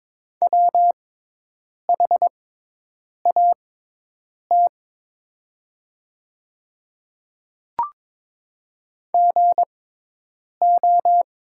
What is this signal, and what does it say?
Morse code, heard as a keyed tone, about 23 wpm, spaced as for 8 wpm, 700 Hz. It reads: WHAT GO
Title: 6x Character Spacing - Top 100 Words - 22wpm